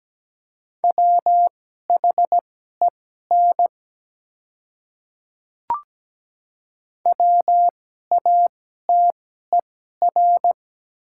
Morse code, heard as a keyed tone, about 17 wpm, spaced as for 12 wpm, 690 Hz. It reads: WHEN WATER
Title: 2x Character Spacing - Top 100 Words - 17wpm